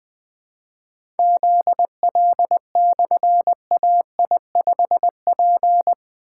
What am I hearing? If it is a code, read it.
Morse code, heard as a keyed tone, about 20 wpm, 700 Hz. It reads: ZL/AI5P
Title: Difficult Call Signs 20wpm